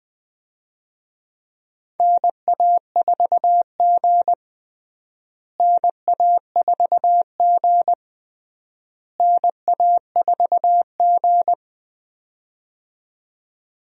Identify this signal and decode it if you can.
Morse code, heard as a keyed tone, about 20 wpm, 700 Hz. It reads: NA4G NA4G NA4G
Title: USA Call Signs - 4 Characters - Repeated 3x - 20wpm